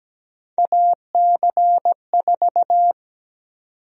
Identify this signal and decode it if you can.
Morse code, heard as a keyed tone, about 17 wpm, 700 Hz. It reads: AC4
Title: USA Call Sign Prefixes - 17wpm